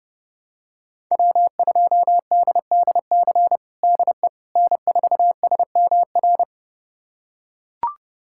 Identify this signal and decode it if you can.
Morse code, heard as a keyed tone, about 30 wpm, 700 Hz. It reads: W2DDC DE N4SMR
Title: USA Callsign DE Callsign - 5 Characters - 30wpm